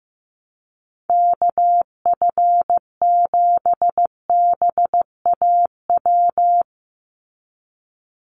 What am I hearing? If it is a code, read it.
Morse code, heard as a keyed tone, about 15 wpm, 700 Hz. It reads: KF7BAW
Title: USA Call Signs 15wpm